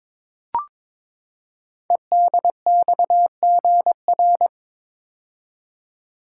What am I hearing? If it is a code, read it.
Morse code, heard as a keyed tone, about 22 wpm, 700 Hz. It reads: EDXGR
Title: Code Groups - 5 Letters - 22wpm